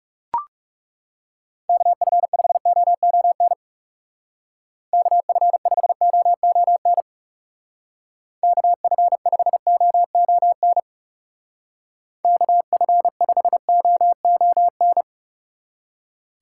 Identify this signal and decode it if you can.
Morse code, heard as a keyed tone, about 36 wpm, 700 Hz. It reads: KF5OON KF5OON KF5OON KF5OON
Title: Speed Racing - USA 2x3 Call Signs 30wpm